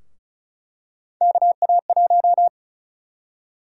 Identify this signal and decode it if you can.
Morse code, heard as a keyed tone, about 36 wpm, 700 Hz. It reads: KA1
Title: USA Call Sign Prefixes - 35wpm